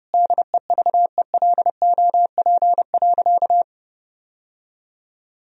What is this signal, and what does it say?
Morse code, heard as a keyed tone, about 30 wpm, 700 Hz. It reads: DEVELOP.
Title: Sentences from Top 1000 Words - Review Part 2 - 30wpm